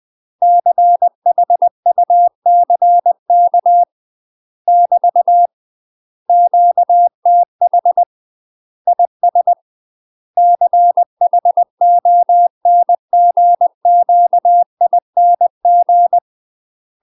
Morse code, heard as a keyed tone, about 20 wpm, 700 Hz. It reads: CHUCK <BT> QTH IS CHONGQING
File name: UK MASTER QSOs TRAINING / 20WPM / UK MASTER QSOs 0062